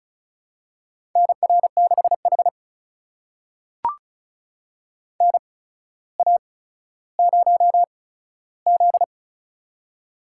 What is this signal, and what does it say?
Morse code, heard as a keyed tone, about 36 wpm, spaced as for 15 wpm, 700 Hz. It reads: NR6H NA0Z
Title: USA Call Signs - 4 Characters - 8x Character Spacing - 35wpm